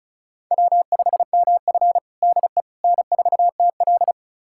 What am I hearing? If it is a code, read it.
Morse code, heard as a keyed tone, about 36 wpm, 700 Hz. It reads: W5MF DE N4TL
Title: USA Callsign DE Callsign - 4 Characters - 35wpm